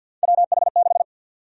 Morse code, heard as a keyed tone, about 53 wpm, 700 Hz. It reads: WH6
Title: USA Call Sign Prefixes - Repeated 2x - 50wpm